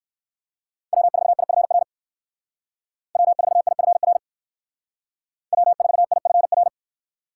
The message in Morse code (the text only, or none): W4IFR W4IFR W4IFR